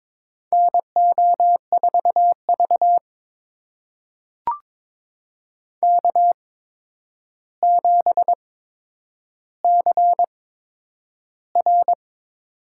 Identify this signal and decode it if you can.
Morse code, heard as a keyed tone, about 22 wpm, spaced as for 9 wpm, 700 Hz. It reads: NO4V K7CR
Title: USA Call Signs - 4 Characters - 8x Character Spacing - 22wpm